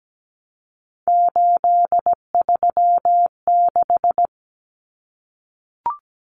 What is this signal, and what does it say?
Morse code, heard as a keyed tone, about 17 wpm, 700 Hz. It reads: 836
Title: Numbers - 3-Digit 17wpm